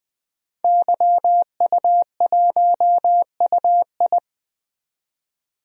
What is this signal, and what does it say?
Morse code, heard as a keyed tone, about 20 wpm, 700 Hz. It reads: YU1UI
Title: DXCC-100 Call Signs - 5 Characters - 20wpm